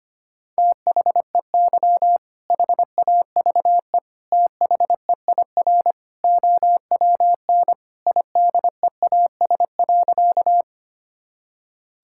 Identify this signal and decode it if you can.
Morse code, heard as a keyed tone, about 25 wpm, 700 Hz. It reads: THEY HAVE THEIR OWN IDEAS.
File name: practice / 25wpm / Sentences with 2-Word Phrases - 25wpm